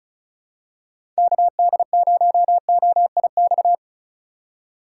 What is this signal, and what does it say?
Morse code, heard as a keyed tone, about 34 wpm, 700 Hz. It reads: KD0OIX
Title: K1USN SST QSO Practice - 35wpm